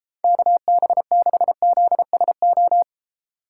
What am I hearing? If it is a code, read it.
Morse code, heard as a keyed tone, about 34 wpm, 700 Hz. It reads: KB6ZSO
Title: Speed Racing - USA 2x3 Call Signs 28wpm